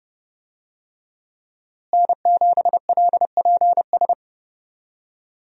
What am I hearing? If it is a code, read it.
Morse code, heard as a keyed tone, about 31 wpm, 700 Hz. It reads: N7LPS
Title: USA Call Signs - 5 Characters - Repeated 3x - 30wpm